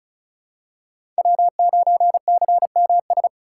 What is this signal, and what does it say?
Morse code, heard as a keyed tone, about 36 wpm, 700 Hz. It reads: W9CMS
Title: USA Call Signs - 5 Characters - 3x Character Spacing - 35wpm